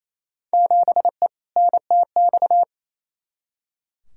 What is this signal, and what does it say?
Morse code, heard as a keyed tone, about 28 wpm, 700 Hz. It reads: 7E NTX
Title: Field Day Exchanges Encore - 28wpm